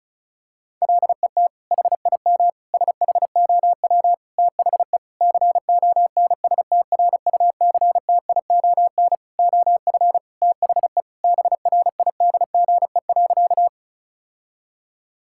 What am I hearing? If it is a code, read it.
Morse code, heard as a keyed tone, about 34 wpm, 700 Hz. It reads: LET HIM SHOW THE CONSTRUCTION OF THE BRIDGE.
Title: Sentences from Top 1600 Words 35wpm